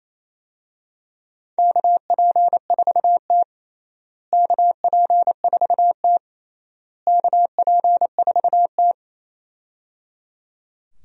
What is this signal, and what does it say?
Morse code, heard as a keyed tone, about 28 wpm, 700 Hz. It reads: KP4T KP4T KP4T
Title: DXCC-100 Call Signs - 4 Characters - Repeated 3x - 28wpm